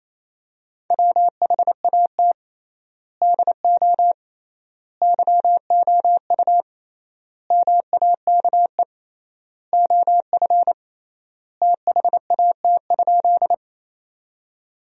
Morse code, heard as a keyed tone, about 28 wpm, 700 Hz. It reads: WHAT DO YOU MAKE OF THAT?
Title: 3x Word-Spacing - Sentences from Top 100 Words - 28wpm